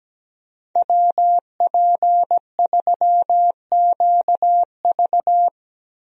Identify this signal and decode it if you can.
Morse code, heard as a keyed tone, about 17 wpm, 700 Hz. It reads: WP3QV
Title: USA Call Signs - 5 Characters - Repeated 3x - 17wpm